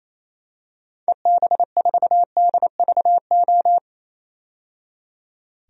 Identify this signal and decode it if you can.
Morse code, heard as a keyed tone, about 27 wpm, 700 Hz. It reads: EB4DVO
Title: DXCC-100 Call Signs - 6 Characters - 28wpm